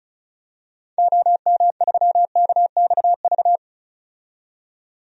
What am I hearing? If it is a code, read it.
Morse code, heard as a keyed tone, about 36 wpm, 700 Hz. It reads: OM3KXV
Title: DXCC-100 Call Signs - 6 Characters - 35wpm